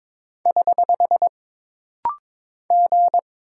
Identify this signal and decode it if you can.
Morse code, heard as a keyed tone, about 22 wpm, 700 Hz. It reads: <HH> G
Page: This rapid-fire timing is geared to push your ICR (Instant Character Recognition) skills.